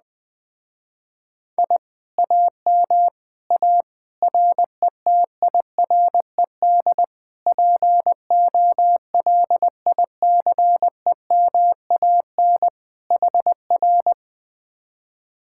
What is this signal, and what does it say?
Morse code, heard as a keyed tone, about 20 wpm, 700 Hz. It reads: I AM A RETIRED POLICEMAN HR
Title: Speed-Racing Common QSO Phrases - 17wpm